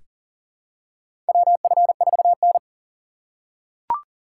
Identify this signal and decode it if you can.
Morse code, heard as a keyed tone, about 40 wpm, 700 Hz. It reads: WF4N
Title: USA Call Signs - 4 Characters - 2x Character Spacing - 40wpm